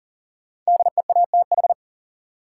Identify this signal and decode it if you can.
Morse code, heard as a keyed tone, about 40 wpm, 700 Hz. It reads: DEATH
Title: Sentences from Top 1100 Words - New Word Drill - 40wpm